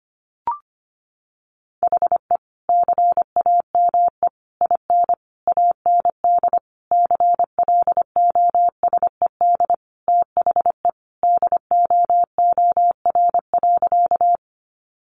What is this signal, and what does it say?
Morse code, heard as a keyed tone, about 25 wpm, 700 Hz. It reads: HE CAME IN AND CLOSED THE DOOR.
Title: Sentences from Top 1400 Words 25wpm